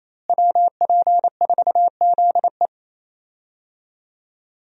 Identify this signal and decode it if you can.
Morse code, heard as a keyed tone, about 28 wpm, 700 Hz. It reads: WP4ZE
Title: USA Call Signs - 5 Characters - Repeated 3x - 28wpm